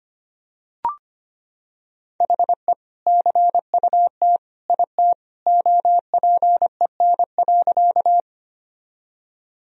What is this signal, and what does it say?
Morse code, heard as a keyed tone, about 25 wpm, 700 Hz. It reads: HE CUT IT OPEN.
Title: Sentences from Top 1000 Words - Review Part 1 - 25wpm